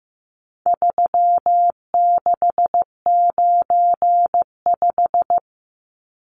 Morse code, heard as a keyed tone, about 15 wpm, 700 Hz. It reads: 3695
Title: Numbers - 4-Digit 15wpm